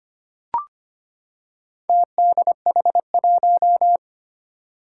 Morse code, heard as a keyed tone, about 25 wpm, 700 Hz. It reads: TDH1
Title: Four Challenging Letter-Number Combinations 25wpm